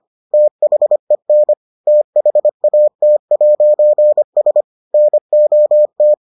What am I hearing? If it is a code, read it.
Morse code, heard as a keyed tone, about 25 wpm, 600 Hz. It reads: THEN THAT'S NOT